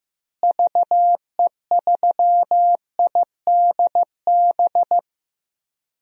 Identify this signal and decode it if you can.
Morse code, heard as a keyed tone, about 15 wpm, 700 Hz. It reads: VE3IDB
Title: DXCC-100 Call Signs - 6 Characters - Repeated 2x - 15wpm